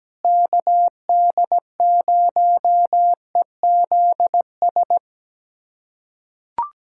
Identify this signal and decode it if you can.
Morse code, heard as a keyed tone, about 17 wpm, 700 Hz. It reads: KD0EZS
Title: USA Callsign DE Callsign 17wpm